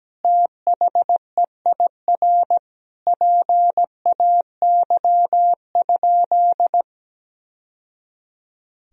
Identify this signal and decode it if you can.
Morse code, heard as a keyed tone, about 17 wpm, 700 Hz. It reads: THEIR PAY?
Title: Sentences from Top 1000 Words - Review Part 4 - 17wpm